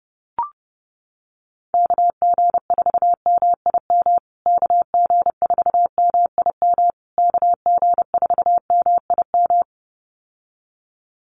Morse code, heard as a keyed tone, about 30 wpm, 700 Hz. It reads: KG4MIM KG4MIM KG4MIM
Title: USA Call Signs Repeated 3x 30wpm